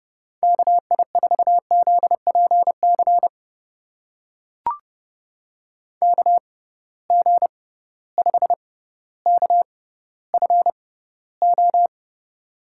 Morse code, heard as a keyed tone, about 30 wpm, spaced as for 15 wpm, 700 Hz. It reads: KI4ZPC KG5KFO
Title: USA Call Signs - 6 Characters - 6x Character Spacing - 30wpm